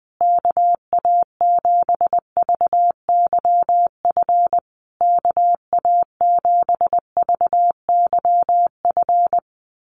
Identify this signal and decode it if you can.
Morse code, heard as a keyed tone, about 20 wpm, 700 Hz. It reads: KA7VYF KA7VYF
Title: USA Call Signs Repeated 3x 20wpm